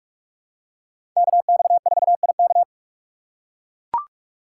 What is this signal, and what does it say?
Morse code, heard as a keyed tone, about 43 wpm, 700 Hz. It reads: KX4IK
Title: USA Call Signs - 4x Character Spacing - 45wpm